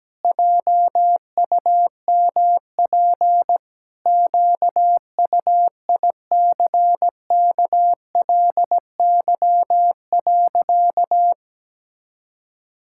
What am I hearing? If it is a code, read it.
Morse code, heard as a keyed tone, about 17 wpm, 700 Hz. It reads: JUMP QUICKLY.
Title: Pangram Challenge v2 - 17wpm